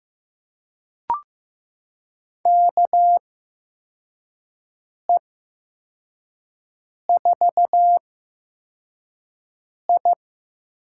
Morse code, heard as a keyed tone, about 15 wpm, spaced as for 4 wpm, 700 Hz. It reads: KE4I
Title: USA Call Signs - 6 Characters - 8x Character Spacing - 15wpm